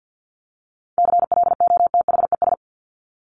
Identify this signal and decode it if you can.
Morse code, heard as a keyed tone, about 50 wpm, 700 Hz. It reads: CLOTHES
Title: Top 1751-2000 Words 50wpm